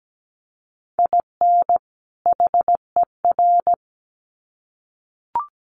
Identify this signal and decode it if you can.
Morse code, heard as a keyed tone, about 17 wpm, 700 Hz. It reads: IN HER
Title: Sets of 2 Words 17wpm